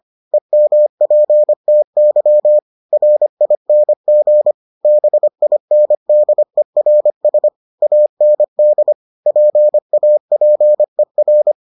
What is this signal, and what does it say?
Morse code, heard as a keyed tone, about 25 wpm, 600 Hz. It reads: EMPTY RING BINDERS AND PAPER